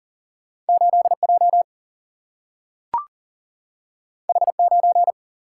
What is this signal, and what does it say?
Morse code, heard as a keyed tone, about 40 wpm, 700 Hz. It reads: 8J H9
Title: Single-Letter Single-Number 40wpm